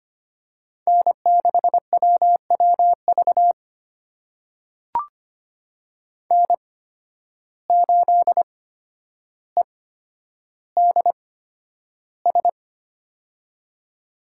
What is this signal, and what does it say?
Morse code, heard as a keyed tone, about 25 wpm, spaced as for 11 wpm, 700 Hz. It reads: N6WWV N8EDS
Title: USA Call Signs - 8x Character Spacing - 25wpm